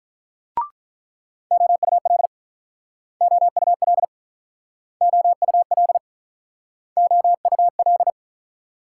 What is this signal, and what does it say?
Morse code, heard as a keyed tone, about 47 wpm, 700 Hz. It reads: OUL OUL OUL OUL